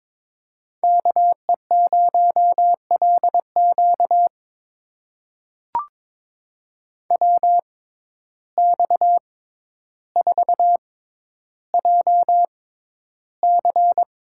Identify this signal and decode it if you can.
Morse code, heard as a keyed tone, about 23 wpm, spaced as for 11 wpm, 700 Hz. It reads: KE0LQ WX4JC
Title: USA Call Signs - 6x Character Spacing - 22wpm